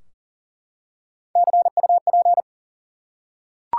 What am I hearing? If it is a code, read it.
Morse code, heard as a keyed tone, about 40 wpm, 700 Hz. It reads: CUP
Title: Three Letter Words - 40wpm